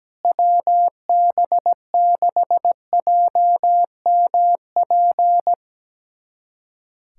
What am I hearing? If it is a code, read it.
Morse code, heard as a keyed tone, about 17 wpm, 700 Hz. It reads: WB6JMP
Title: USA Callsign DE Callsign 17wpm